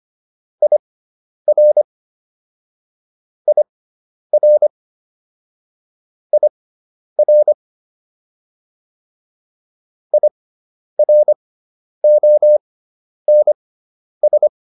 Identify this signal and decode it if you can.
Morse code, heard as a keyed tone, about 25 wpm, spaced as for 10 wpm, 600 Hz. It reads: IR IR IR IRONS